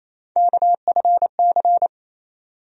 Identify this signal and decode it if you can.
Morse code, heard as a keyed tone, about 28 wpm, 700 Hz. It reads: KFC
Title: USA Call Sign Postfixes - 3 Letters 28wpm